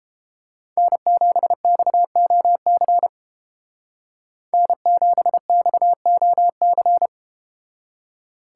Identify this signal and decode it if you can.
Morse code, heard as a keyed tone, about 31 wpm, 700 Hz. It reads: N7XOC N7XOC